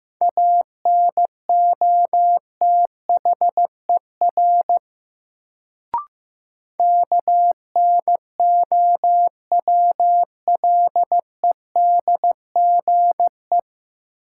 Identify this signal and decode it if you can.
Morse code, heard as a keyed tone, about 15 wpm, 700 Hz. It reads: ANOTHER KNOWLEDGE
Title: Sentences from Top 600 Words - New Word Drill - 15wpm